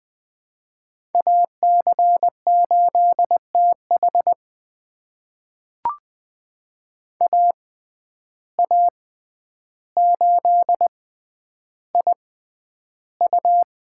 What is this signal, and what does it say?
Morse code, heard as a keyed tone, about 21 wpm, spaced as for 10 wpm, 700 Hz. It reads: AC8TH AA8IU
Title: USA Call Signs - 5 Characters - 6x Character Spacing - 20wpm